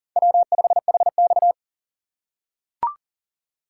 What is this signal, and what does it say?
Morse code, heard as a keyed tone, about 40 wpm, 700 Hz. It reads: W5HX
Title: USA Call Signs - 4 Characters - 2x Character Spacing - 40wpm